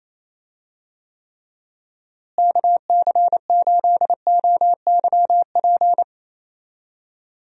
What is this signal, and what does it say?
Morse code, heard as a keyed tone, about 28 wpm, 700 Hz. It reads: KC8OYP
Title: USA Call Signs - 6 Characters - Repeated 2x - 28wpm